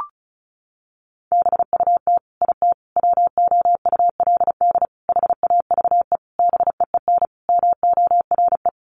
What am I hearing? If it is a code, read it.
Morse code, heard as a keyed tone, about 34 wpm, 700 Hz. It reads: BUT IT WOULD HAVE BEEN MORE
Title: Sentences from Top 1500 Words 35wpm